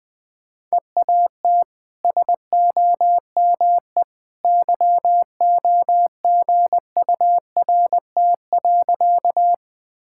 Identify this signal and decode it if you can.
Morse code, heard as a keyed tone, about 20 wpm, 700 Hz. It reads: EAT SOME YOGURT.
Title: Easy Sentences with Hints - 20wpm